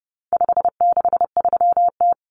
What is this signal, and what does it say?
Morse code, heard as a keyed tone, about 30 wpm, 700 Hz. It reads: 563T
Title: Common Cut Numbers - 4-Digit 30wpm